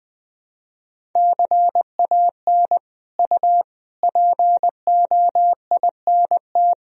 Morse code, heard as a keyed tone, about 21 wpm, 700 Hz. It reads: CAN U POINT
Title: Sentences from Top 200 Words and Abbreviations 20wpm